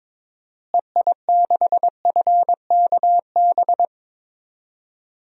Morse code, heard as a keyed tone, about 22 wpm, 700 Hz. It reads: EI6FKB
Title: DXCC-100 Call Signs - 6 Characters - Repeated 2x - 22wpm